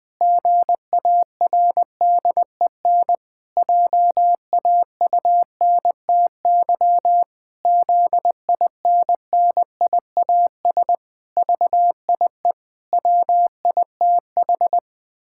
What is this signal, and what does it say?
Morse code, heard as a keyed tone, about 21 wpm, 700 Hz. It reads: GARDEN JAUNTY ZINNIAS VIE WITH
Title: Pangram Challenge v2 - 20wpm